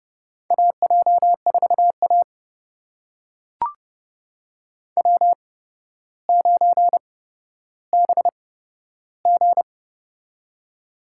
Morse code, heard as a keyed tone, about 30 wpm, spaced as for 13 wpm, 700 Hz. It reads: AJ4A W9BG
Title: USA Call Signs - 4 Characters - 8x Character Spacing - 30wpm